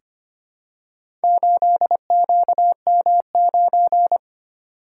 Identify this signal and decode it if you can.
Morse code, heard as a keyed tone, about 25 wpm, 700 Hz. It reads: 8QM9
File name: practice / 25wpm / Four Letter-Number Combinations - 25wpm